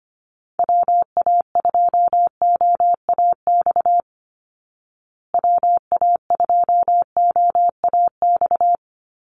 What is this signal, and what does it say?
Morse code, heard as a keyed tone, about 25 wpm, 700 Hz. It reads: WA2OAX WA2OAX
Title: Call Signs Repeated 2x - 25wpm